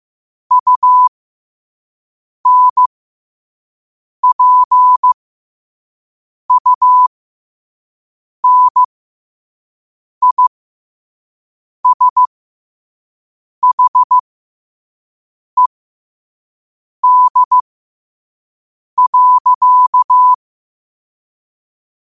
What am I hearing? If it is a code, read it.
Morse code, heard as a keyed tone, about 15 wpm, spaced as for 5 wpm, 1000 Hz. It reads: UNPUNISHED.
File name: qotdfiles / 5wpm